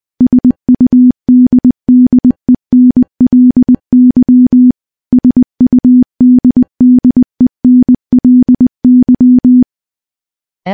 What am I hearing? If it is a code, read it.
Morse code, heard as a keyed tone, about 20 wpm, 250 Hz. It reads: SUDDENLY SUDDENLY
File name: morse / morse-training-20WPM